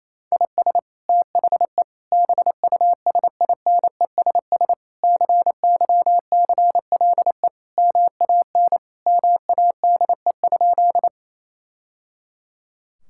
Morse code, heard as a keyed tone, about 28 wpm, 700 Hz. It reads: IS THE BUSINESS CYCLE MAN MADE?
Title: Sentences from Top 1000 Words - Review Part 6 - 28wpm